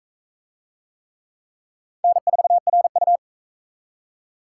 Morse code, heard as a keyed tone, about 44 wpm, 700 Hz. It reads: N4RU